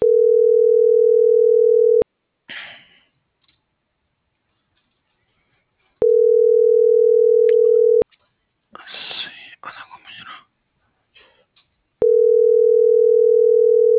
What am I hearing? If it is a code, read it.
Ringback tone (North America), 440 + 480 Hz.